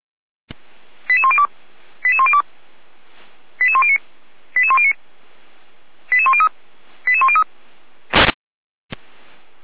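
Five-tone selective call (ZVEI), heard as quick runs of five tones, tones of about 70 ms. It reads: 88102 88102 88109 88109 88103 88103
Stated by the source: ZVEI five-tone selective call received on SDR.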